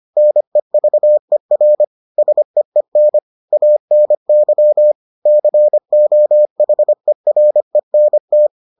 Morse code, heard as a keyed tone, about 25 wpm, 600 Hz. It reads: NEVER SEEN ANY COHERENT